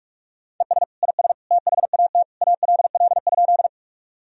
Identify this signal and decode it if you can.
Morse code, heard as a keyed tone, about 46 wpm, 700 Hz. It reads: ES IS THAT ALL?
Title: Sentences from Top 100 Words and Abbreviations 45wpm